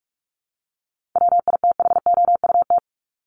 Morse code, heard as a keyed tone, about 44 wpm, 700 Hz. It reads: WITHOUT
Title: Top 501-750 Words 45wpm